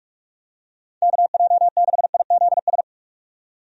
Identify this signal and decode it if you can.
Morse code, heard as a keyed tone, about 41 wpm, 700 Hz. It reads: KJ6IZS
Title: USA Call Signs - 6 Characters - 45wpm